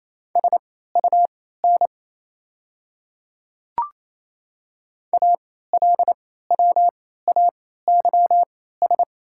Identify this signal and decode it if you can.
Morse code, heard as a keyed tone, about 28 wpm, spaced as for 16 wpm, 700 Hz. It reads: SUN ALWAYS
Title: 3x Character Spacing - Top 300 Words - 28wpm